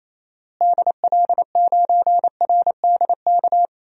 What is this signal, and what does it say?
Morse code, heard as a keyed tone, about 28 wpm, 700 Hz. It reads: DL9RDK
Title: DXCC-100 Call Signs - 6 Characters - Repeated 3x - 28wpm